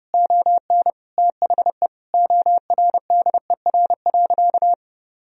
Morse code, heard as a keyed tone, about 30 wpm, 700 Hz. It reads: ON THE ORDER.
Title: Sentences from Top 1000 Words - Review Part 2 - 30wpm